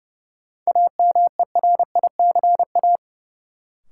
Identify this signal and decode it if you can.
Morse code, heard as a keyed tone, about 30 wpm, 700 Hz. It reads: AMERICA
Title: Sets of 5 Words - New Word Drill - 30wpm